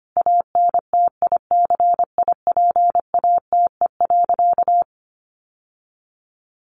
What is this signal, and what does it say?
Morse code, heard as a keyed tone, about 25 wpm, 700 Hz. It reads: ANTICIPATE.
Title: Sentences from Top 1500 Words 25wpm